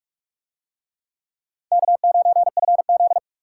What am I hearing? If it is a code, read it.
Morse code, heard as a keyed tone, about 43 wpm, 700 Hz. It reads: K9FZ